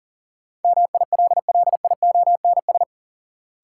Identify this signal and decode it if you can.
Morse code, heard as a keyed tone, about 40 wpm, 700 Hz. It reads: MILLIONS
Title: Sets of 5 Words - New Word Drill - 40wpm